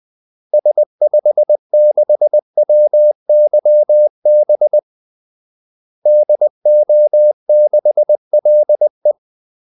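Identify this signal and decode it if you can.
Morse code, heard as a keyed tone, about 20 wpm, 600 Hz. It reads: S56WYB DO6LE
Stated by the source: Morse call-sign practice at 20 wpm.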